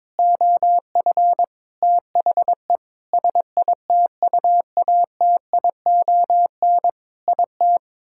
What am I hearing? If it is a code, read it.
Morse code, heard as a keyed tone, about 22 wpm, 700 Hz. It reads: OF THE SITUATION IT
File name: Sentences with 9-Word Phrases - 22wpm